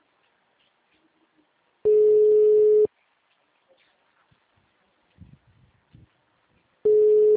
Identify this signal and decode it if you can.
Ringback tone (Europe, 425 Hz).